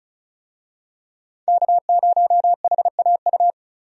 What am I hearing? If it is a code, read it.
Morse code, heard as a keyed tone, about 36 wpm, 700 Hz. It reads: K0HAU
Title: USA Call Signs - 5 Characters - 35wpm